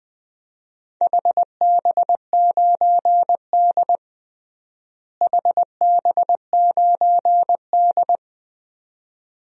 Morse code, heard as a keyed tone, about 20 wpm, 700 Hz. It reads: HB9D HB9D
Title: DXCC-100 Call Signs - 4 Characters - Repeated 2x - 20wpm